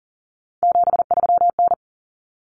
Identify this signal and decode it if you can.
Morse code, heard as a keyed tone, about 40 wpm, 700 Hz. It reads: 73N
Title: Common Cut Numbers - 3-Digit 40wpm